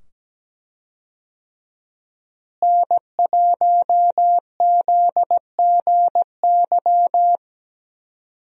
Morse code, heard as a keyed tone, about 17 wpm, 700 Hz. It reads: N1ZGY
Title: USA Call Signs - 5 Characters - Repeated 2x - 17wpm